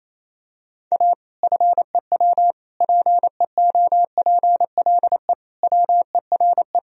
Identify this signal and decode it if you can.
Morse code, heard as a keyed tone, about 27 wpm, 700 Hz. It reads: A FEW PEOPLE WERE
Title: Sentences from Top 1000 Words - Review Part 3 - 28wpm